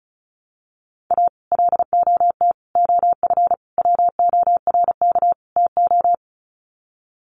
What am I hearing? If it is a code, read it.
Morse code, heard as a keyed tone, about 34 wpm, 700 Hz. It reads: A LOT OF WORK TO
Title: Sets of 5 Words Encore2 35wpm